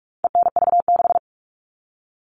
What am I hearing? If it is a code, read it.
Morse code, heard as a keyed tone, about 44 wpm, 700 Hz. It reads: ENV6
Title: All Cut Numbers - 4-Digit 45wpm